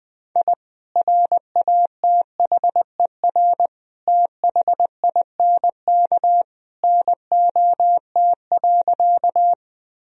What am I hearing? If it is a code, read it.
Morse code, heard as a keyed tone, about 21 wpm, 700 Hz. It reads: I RATHER THINK NOT.
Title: Sentences from Top 1000 Words - Review Part 4 - 20wpm